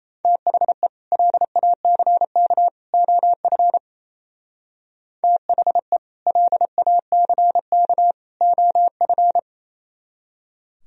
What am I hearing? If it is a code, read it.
Morse code, heard as a keyed tone, about 37 wpm, 700 Hz. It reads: THE LACK OF THE LACK OF